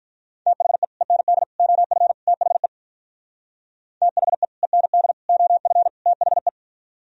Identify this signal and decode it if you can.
Morse code, heard as a keyed tone, about 43 wpm, 700 Hz. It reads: THE END OF THE THE END OF THE
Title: Speed-Racing - Sets of 4 Words 35wpm